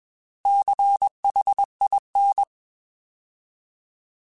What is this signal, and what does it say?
Morse code, heard as a keyed tone, about 21 wpm, 800 Hz. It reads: CHIN